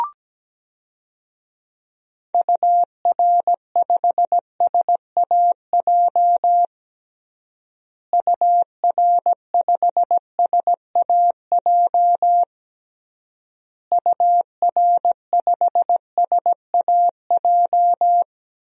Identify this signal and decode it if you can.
Morse code, heard as a keyed tone, about 17 wpm, 700 Hz. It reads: UR5SAJ UR5SAJ UR5SAJ
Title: DXCC-100 Call Signs - 6 Characters - Repeated 3x - 17wpm